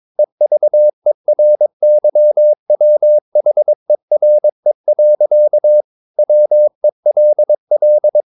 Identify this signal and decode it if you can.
Morse code, heard as a keyed tone, about 22 wpm, 600 Hz. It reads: EVERYWHERE. WELL
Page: Romanen Never af Ken Follet på engelsk.